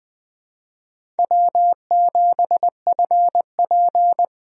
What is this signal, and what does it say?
Morse code, heard as a keyed tone, about 20 wpm, 700 Hz. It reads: W7FP
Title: USA Call Signs - 4 Characters - Repeated 3x - 20wpm